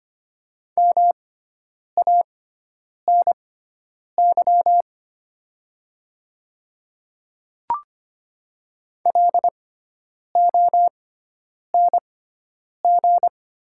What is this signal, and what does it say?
Morse code, heard as a keyed tone, about 25 wpm, spaced as for 9 wpm, 700 Hz. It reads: MANY LONG
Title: 6x Character Spacing - Top 100 Words - 25wpm